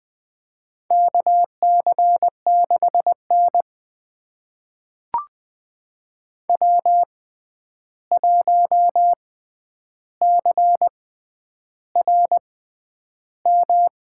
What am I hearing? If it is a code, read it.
Morse code, heard as a keyed tone, about 21 wpm, spaced as for 10 wpm, 700 Hz. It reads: KC6N W1CRM
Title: USA Call Signs - 6x Character Spacing - 20wpm